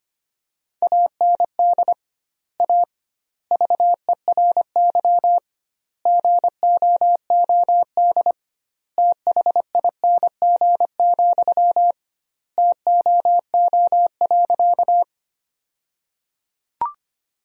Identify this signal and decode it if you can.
Morse code, heard as a keyed tone, about 25 wpm, 700 Hz. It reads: AND A VERY GOOD THING, TOO.
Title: 2x Word-Spacing - Sentences from Top 200 Words - 25wpm